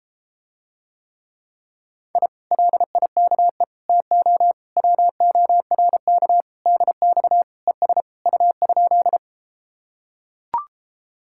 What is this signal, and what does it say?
Morse code, heard as a keyed tone, about 33 wpm, 700 Hz. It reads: I LIKE TO WORK DX ES U?